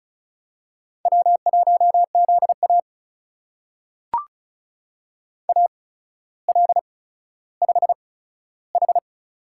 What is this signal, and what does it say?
Morse code, heard as a keyed tone, about 36 wpm, spaced as for 15 wpm, 700 Hz. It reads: W1ZA AL5H
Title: USA Call Signs - 4 Characters - 8x Character Spacing - 35wpm